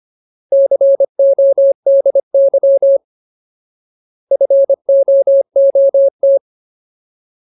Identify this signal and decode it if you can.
Morse code, heard as a keyed tone, about 25 wpm, 550 Hz. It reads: CODY FOOT
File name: Ultimate Word List / 25WPM Ultimate